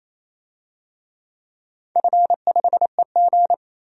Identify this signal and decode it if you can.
Morse code, heard as a keyed tone, about 28 wpm, 700 Hz. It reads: F5EG
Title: DXCC-100 Call Signs - 4 Characters - 28wpm